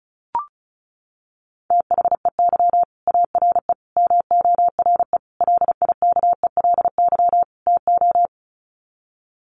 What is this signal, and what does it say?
Morse code, heard as a keyed tone, about 34 wpm, 700 Hz. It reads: THEY ARE MORE LIKELY TO
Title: Sets of 5 Words Encore2 35wpm